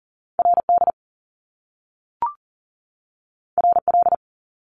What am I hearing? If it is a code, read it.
Morse code, heard as a keyed tone, about 40 wpm, 700 Hz. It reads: RD RL
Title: Top Two Letter Combinations in Words 40wpm